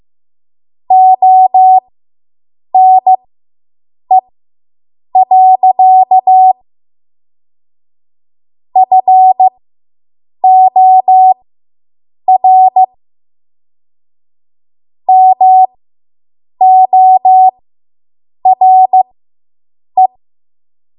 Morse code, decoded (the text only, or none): ONE. FOR MORE